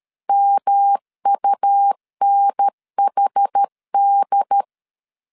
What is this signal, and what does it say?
Morse code, heard as a keyed tone, about 13 wpm, 800 Hz. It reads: MUNHD
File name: CW / DET / 1 3 1500C S B 12wpm 2828